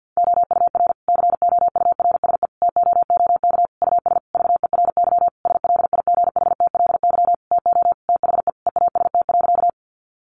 Morse code, heard as a keyed tone, about 50 wpm, 700 Hz. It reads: OUR COURSE TOOK US VERY SLIGHTLY TO THE EAST.
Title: Sentences from Top 1300 Words 50wpm